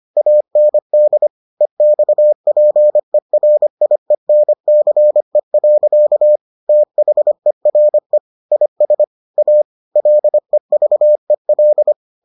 Morse code, heard as a keyed tone, about 25 wpm, 600 Hz. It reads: AND EXPERIENCE. THERE IS A LEVEL